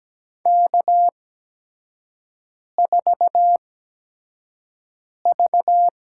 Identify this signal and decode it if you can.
Morse code, heard as a keyed tone, about 17 wpm, spaced as for 5 wpm, 700 Hz. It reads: K4V
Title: USA Call Signs - 5 Characters - 8x Character Spacing - 17wpm